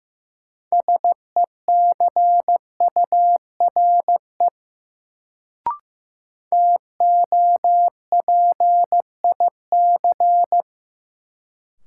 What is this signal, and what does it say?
Morse code, heard as a keyed tone, about 15 wpm, 700 Hz. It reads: SECURE TOPIC
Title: Sentences from Top 1200 Words - New Word Drill - 15wpm